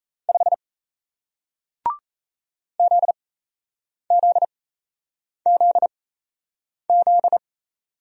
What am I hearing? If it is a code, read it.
Morse code, heard as a keyed tone, about 37 wpm, spaced as for 8 wpm, 700 Hz. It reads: 5 ZZZZ